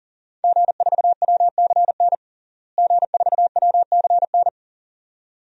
Morse code, heard as a keyed tone, about 37 wpm, 700 Hz. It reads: G4WCN G4WCN
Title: DXCC-100 Call Signs - 5 Characters - Repeated 2x - 40wpm